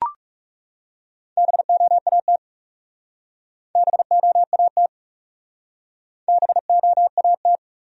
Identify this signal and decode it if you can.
Morse code, heard as a keyed tone, about 40 wpm, 700 Hz. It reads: BOAT BOAT BOAT